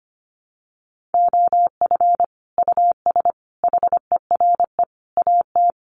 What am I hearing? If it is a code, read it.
Morse code, heard as a keyed tone, about 25 wpm, 700 Hz. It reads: OF US HERE AT
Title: Sets of 5 Words 25wpm